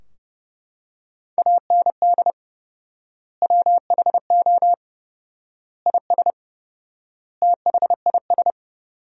Morse code, heard as a keyed tone, about 32 wpm, 700 Hz. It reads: AND WHO IS THIS
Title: 4x Word-Spacing - Sentences from Top 300 Words - 30wpm